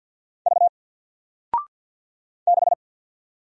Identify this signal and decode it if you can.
Morse code, heard as a keyed tone, about 48 wpm, 700 Hz. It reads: V 6
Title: Single Character (Letter:Number:Punctuation:Prosign) 50wpm